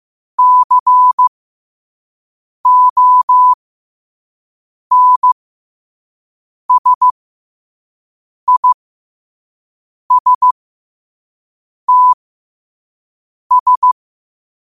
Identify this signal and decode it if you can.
Morse code, heard as a keyed tone, about 15 wpm, spaced as for 5 wpm, 1000 Hz. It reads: CONSISTS
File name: qotdfiles / 5wpm